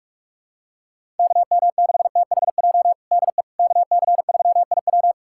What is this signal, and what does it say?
Morse code, heard as a keyed tone, about 42 wpm, 700 Hz. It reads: KM6THJ DE KC3IW
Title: USA Callsign DE Callsign 45wpm